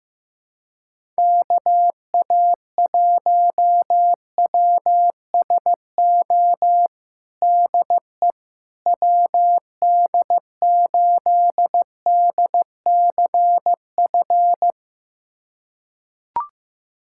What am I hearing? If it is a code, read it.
Morse code, heard as a keyed tone, about 15 wpm, 700 Hz. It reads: KA1WSO DE WD8DCF
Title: USA Callsign DE Callsign - 6 Characters - 15wpm